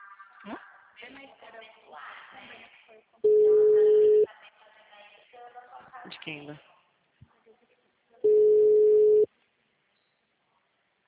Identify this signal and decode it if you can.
Ringback tone (Europe, 425 Hz), with speech.
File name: out- -2002- - - .58722